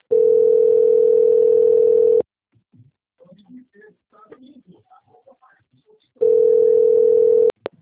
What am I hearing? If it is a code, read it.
Ringback tone (North America), 440 + 480 Hz.